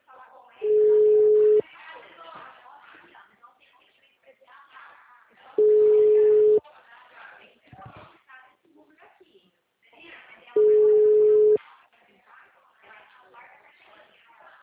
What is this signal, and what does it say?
Ringback tone (Europe, 425 Hz), with speech.